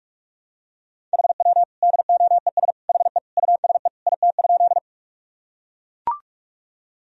Morse code, heard as a keyed tone, about 42 wpm, 700 Hz. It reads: HW DOES HE USE IT?
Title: Sentences from Top 200 Words and Abbreviations 45wpm